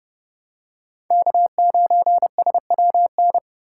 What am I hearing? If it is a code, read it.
Morse code, heard as a keyed tone, about 28 wpm, 700 Hz. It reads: K9SWN